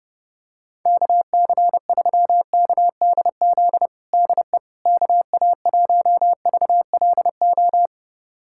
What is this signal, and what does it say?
Morse code, heard as a keyed tone, about 32 wpm, 700 Hz. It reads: KC3KDZ DE KA1VLO
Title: USA Callsign DE Callsign - 6 Characters - 30wpm